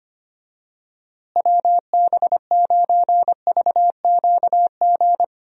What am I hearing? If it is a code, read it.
Morse code, heard as a keyed tone, about 25 wpm, 700 Hz. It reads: WB9VQG
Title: USA Call Signs - 6 Characters - Repeated 2x - 25wpm